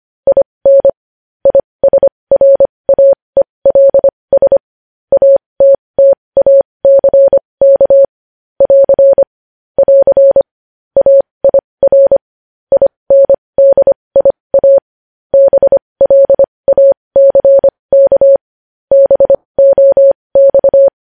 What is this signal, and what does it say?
Morse code, heard as a keyed tone, about 25 wpm, spaced as for 20 wpm, 550 Hz. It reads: IN ISRAELS ATTACK <AR> <AR> AIR INDIA BLACK BOX
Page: News Headlines in Morse Code at 20 WPM